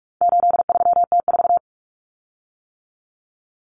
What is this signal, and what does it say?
Morse code, heard as a keyed tone, about 44 wpm, 700 Hz. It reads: 83T4
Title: Common Cut Numbers - 4-Digit 45wpm